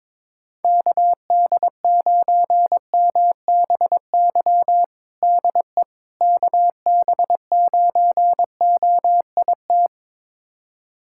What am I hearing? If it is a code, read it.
Morse code, heard as a keyed tone, about 22 wpm, 700 Hz. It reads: KD9MBY DE KB9OIT
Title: USA Callsign DE Callsign 22wpm